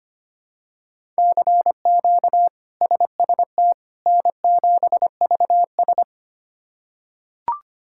Morse code, heard as a keyed tone, about 25 wpm, 700 Hz. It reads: CQ SST N7VS
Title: K1USN SST QSO Practice - 25wpm